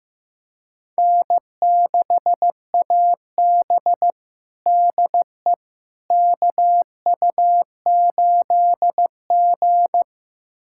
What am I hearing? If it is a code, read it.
Morse code, heard as a keyed tone, about 15 wpm, 700 Hz. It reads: N6AB DE KU8G
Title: USA Callsign DE Callsign - 4 Characters - 15wpm